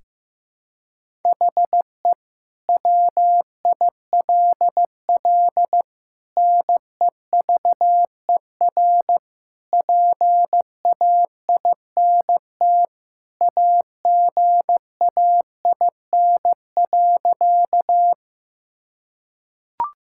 Morse code, heard as a keyed tone, about 15 wpm, 700 Hz. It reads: HE WILL NEVER PAINT AGAIN.
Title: Sentences from Top 1000 Words - Review Part 6 - 15wpm